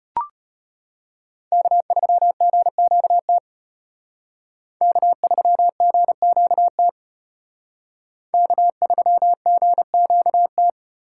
Morse code, heard as a keyed tone, about 30 wpm, 700 Hz. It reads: K3GQT K3GQT K3GQT
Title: Speed Racing - USA 1x3 Call Signs 25wpm